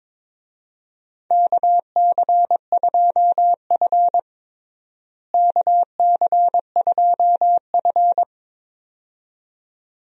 Morse code, heard as a keyed tone, about 22 wpm, 700 Hz. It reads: KC2F KC2F
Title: USA Call Signs - 4 Characters - Repeated 2x - 22wpm